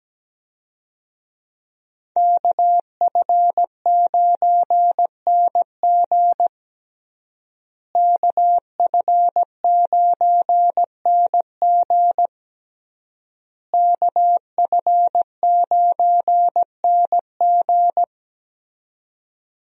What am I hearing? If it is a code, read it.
Morse code, heard as a keyed tone, about 17 wpm, 700 Hz. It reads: KF9NG KF9NG KF9NG
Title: USA Call Signs - 5 Characters - Repeated 3x - 17wpm